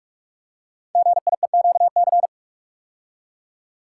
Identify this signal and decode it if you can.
Morse code, heard as a keyed tone, about 44 wpm, 700 Hz. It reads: GIEQC
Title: Code Groups - 5 Letter:Numbers - 45wpm